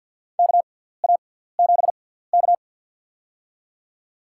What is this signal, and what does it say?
Morse code, heard as a keyed tone, about 53 wpm, spaced as for 17 wpm, 700 Hz. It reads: KA7K
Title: USA Call Signs - 6x Character Spacing - 50wpm